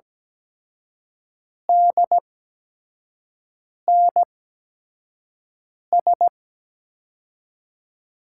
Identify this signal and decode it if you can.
Morse code, heard as a keyed tone, about 17 wpm, spaced as for 5 wpm, 700 Hz. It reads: DNS
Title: USA Call Signs - 8x Character Spacing - 17wpm